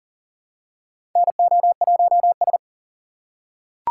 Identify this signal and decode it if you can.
Morse code, heard as a keyed tone, about 40 wpm, 700 Hz. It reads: NO1S
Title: USA Call Signs - 3x Character Spacing - 40wpm